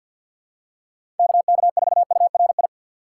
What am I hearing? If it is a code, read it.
Morse code, heard as a keyed tone, about 47 wpm, 700 Hz. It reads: KK4URI
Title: USA Call Signs - 4x Character Spacing - 50wpm